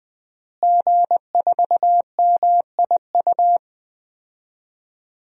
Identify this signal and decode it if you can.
Morse code, heard as a keyed tone, about 20 wpm, 700 Hz. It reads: G4MIU
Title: DXCC-100 Call Signs - 5 Characters - Repeated 2x - 20wpm